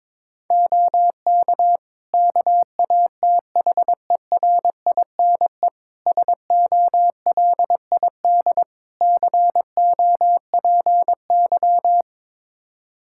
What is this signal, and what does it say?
Morse code, heard as a keyed tone, about 22 wpm, 700 Hz. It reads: OK KATHERINE SOLID COPY
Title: Practice QSO - Exchange 3 - More Info - 22wpm